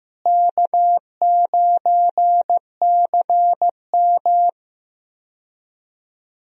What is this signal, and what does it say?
Morse code, heard as a keyed tone, about 15 wpm, 700 Hz. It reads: K9CM
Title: USA Call Signs - 4 Characters - Repeated 2x - 15wpm